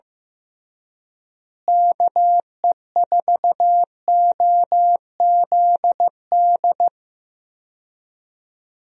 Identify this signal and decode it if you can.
Morse code, heard as a keyed tone, about 15 wpm, 700 Hz. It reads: KE4OZD
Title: USA Call Signs - 6 Characters - Repeated 3x - 15wpm